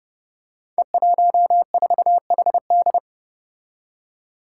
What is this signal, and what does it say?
Morse code, heard as a keyed tone, about 30 wpm, 700 Hz. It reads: E14HD
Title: Five Challenging Letter-Number Combinations 30wpm